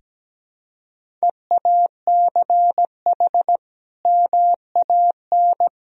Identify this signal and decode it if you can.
Morse code, heard as a keyed tone, about 17 wpm, 700 Hz. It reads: EACH MAN
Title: Sentences from Top 1000 Words - Review Part 2 - 17wpm